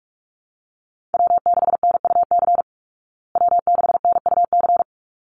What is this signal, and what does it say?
Morse code, heard as a keyed tone, about 44 wpm, 700 Hz. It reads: W6NUC W6NUC
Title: Call Signs Repeated 3x - 45wpm